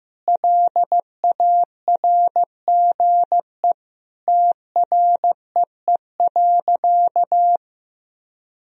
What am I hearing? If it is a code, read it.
Morse code, heard as a keyed tone, about 15 wpm, 700 Hz. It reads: LARGE TREE.
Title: Sentences from Top 1000 Words - Review Part 4 - 15wpm